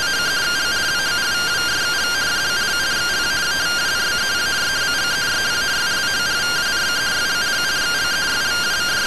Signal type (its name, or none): two-tone teleprinter signal